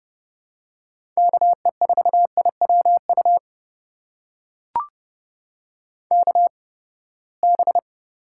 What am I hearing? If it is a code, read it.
Morse code, heard as a keyed tone, about 30 wpm, spaced as for 21 wpm, 700 Hz. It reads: KE4IWU KB
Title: USA Call Signs - 6 Characters - 8x Character Spacing - 30wpm